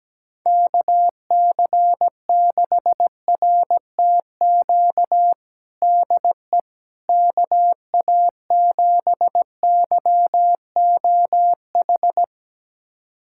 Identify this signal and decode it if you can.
Morse code, heard as a keyed tone, about 17 wpm, 700 Hz. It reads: KC6RTQ DE KA7YOH
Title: USA Callsign DE Callsign - 6 Characters - 17wpm